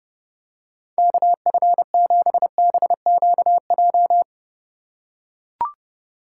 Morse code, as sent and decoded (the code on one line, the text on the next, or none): -.- ..-. --... -... --.- .---
KF7BQJ